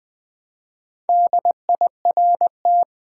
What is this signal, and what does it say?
Morse code, heard as a keyed tone, about 20 wpm, 700 Hz. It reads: DIRT
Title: Build A Word - 1 Letter At A Time - 5-Letter Words - 20wpm